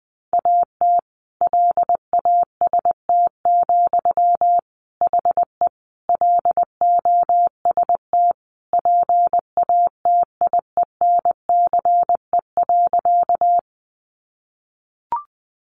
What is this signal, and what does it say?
Morse code, heard as a keyed tone, about 20 wpm, 700 Hz. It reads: AT LAST, HE LOST PATIENCE.
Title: Sentences from Top 1400 Words 20wpm